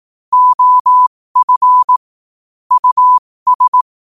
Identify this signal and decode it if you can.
Morse code, heard as a keyed tone, about 18 wpm, spaced as for 15 wpm, 1000 Hz. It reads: OF US
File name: qotdfiles / 15wpm